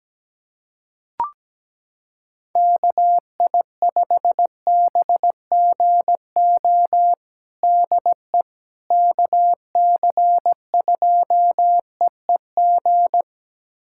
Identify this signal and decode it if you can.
Morse code, heard as a keyed tone, about 17 wpm, 700 Hz. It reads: KI5BGO DE KC2EEG
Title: USA Callsign DE Callsign 17wpm